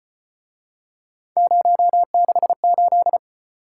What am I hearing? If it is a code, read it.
Morse code, heard as a keyed tone, about 35 wpm, 700 Hz. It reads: 068